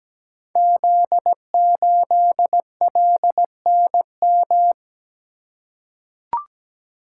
Morse code, heard as a keyed tone, about 17 wpm, 700 Hz. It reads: Z8LNM
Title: Code Groups - 5 Letter:Numbers - 17wpm